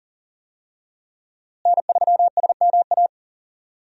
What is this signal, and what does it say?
Morse code, heard as a keyed tone, about 40 wpm, 700 Hz. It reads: N3SMA
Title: USA Call Signs - 5 Characters - Repeated 2x - 40wpm